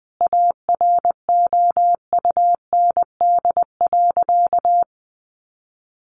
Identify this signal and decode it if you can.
Morse code, heard as a keyed tone, about 20 wpm, 700 Hz. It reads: AROUND.
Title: Sentences from Top 600 Words 20wpm